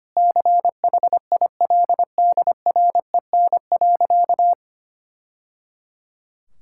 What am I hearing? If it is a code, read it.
Morse code, heard as a keyed tone, about 25 wpm, 700 Hz. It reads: CHILDREN.
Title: Sentences from Top 1000 Words - Review Part 6 - 25wpm